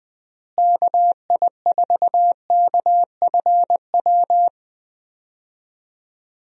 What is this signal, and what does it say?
Morse code, heard as a keyed tone, about 20 wpm, 700 Hz. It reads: KI4KFW
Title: USA Call Signs - 6 Characters - Repeated 3x - 20wpm